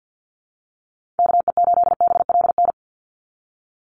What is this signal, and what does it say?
Morse code, heard as a keyed tone, about 50 wpm, 700 Hz. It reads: KE8BLN
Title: USA Call Signs 50wpm